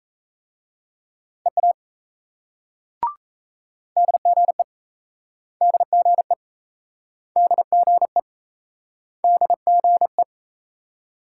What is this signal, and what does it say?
Morse code, heard as a keyed tone, about 38 wpm, 700 Hz. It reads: EA DGE DGE DGE DGE